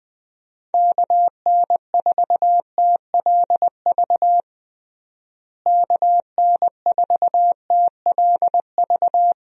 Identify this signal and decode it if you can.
Morse code, heard as a keyed tone, about 20 wpm, 700 Hz. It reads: KN4TLV KN4TLV
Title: USA Call Signs - 6 Characters - Repeated 2x - 20wpm